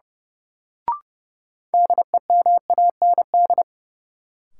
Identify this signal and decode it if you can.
Morse code, heard as a keyed tone, about 30 wpm, 700 Hz. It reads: DEMAND
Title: Sentences from Top 800 Words - New Word Drill - 30wpm